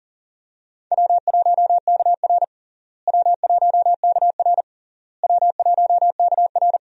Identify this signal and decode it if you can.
Morse code, heard as a keyed tone, about 40 wpm, 700 Hz. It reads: W1KR W1KR W1KR
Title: USA Call Signs - 4 Characters - Repeated 3x - 40wpm